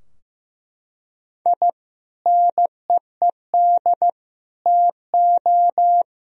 Morse code, heard as a keyed tone, about 16 wpm, 700 Hz. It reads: I NEED TO
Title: Sentences from Top 1000 Words - Review Part 4 - 15wpm